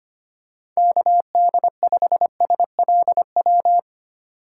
Morse code, heard as a keyed tone, about 25 wpm, 700 Hz. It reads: KD5SLW
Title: USA Call Signs - 3x Character Spacing - 25wpm